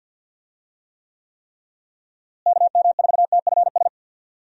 Morse code, heard as a keyed tone, about 51 wpm, 700 Hz. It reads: KM4TFS
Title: USA Call Signs - 6 Characters - Repeated 3x - 50wpm